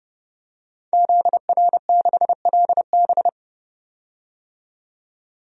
Morse code, heard as a keyed tone, about 30 wpm, 700 Hz. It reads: ZR6LB
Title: DXCC-100 Call Signs - 5 Characters - 30wpm